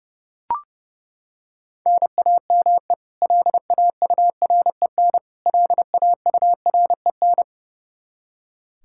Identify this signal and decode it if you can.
Morse code, heard as a keyed tone, about 30 wpm, 700 Hz. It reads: NAME LAUREN LAUREN
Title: Practice QSO - Exchange 1 - Beginning - 30wpm